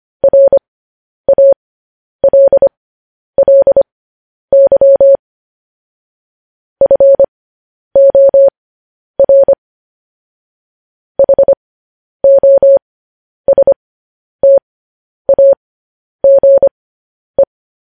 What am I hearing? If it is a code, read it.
Morse code, heard as a keyed tone, about 25 wpm, spaced as for 10 wpm, 550 Hz. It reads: RALLY FOR HOSTAGE
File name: podcast / News headlines-1x-25x10